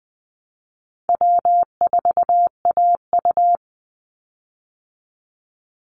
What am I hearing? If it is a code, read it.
Morse code, heard as a keyed tone, about 19 wpm, 700 Hz. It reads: W4AU
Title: Easy Call Signs Repeated 2x - 20wpm